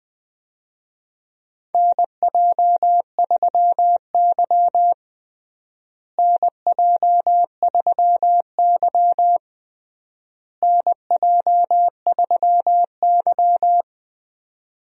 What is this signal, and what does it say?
Morse code, heard as a keyed tone, about 20 wpm, 700 Hz. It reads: NJ3Y NJ3Y NJ3Y
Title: USA Call Signs - 4 Characters - Repeated 3x - 20wpm